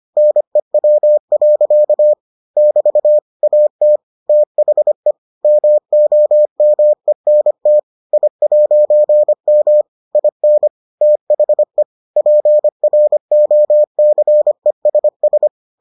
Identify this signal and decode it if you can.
Morse code, heard as a keyed tone, about 25 wpm, 600 Hz. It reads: NEW. <BT> AT THE MOMENT I'M IN THE PROCESS